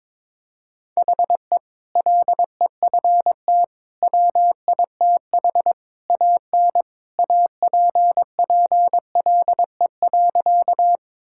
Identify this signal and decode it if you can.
Morse code, heard as a keyed tone, about 22 wpm, 700 Hz. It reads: HE LEFT WITH AN APPLE.
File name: practice / 22wpm / Sentences with 2-Word Phrases - 22wpm